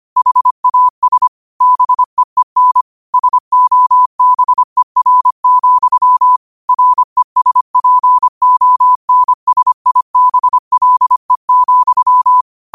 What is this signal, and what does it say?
Morse code, heard as a keyed tone, about 25 wpm, 1000 Hz. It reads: HAS BEEN SOBER, RESPONSIBLE,
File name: qotdfiles / 25wpm